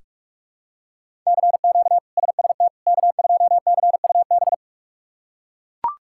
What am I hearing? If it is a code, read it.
Morse code, heard as a keyed tone, about 42 wpm, 700 Hz. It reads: CQ SST K2CUB
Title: K1USN SST QSO Practice - 45wpm